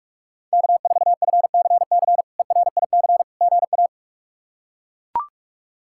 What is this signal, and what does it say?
Morse code, heard as a keyed tone, about 41 wpm, 700 Hz. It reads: K4FCC ERIC GA
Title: K1USN SST QSO Practice - 45wpm